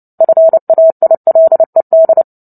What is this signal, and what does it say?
Morse code, heard as a keyed tone, about 29 wpm, 650 Hz. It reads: FAILED
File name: Book Two / 03-Chapter 3 - 29 WPM